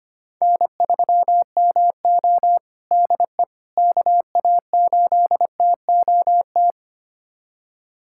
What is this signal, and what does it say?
Morse code, heard as a keyed tone, about 25 wpm, 700 Hz. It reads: N3MO DE KA8TOT
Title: USA Callsign DE Callsign 25wpm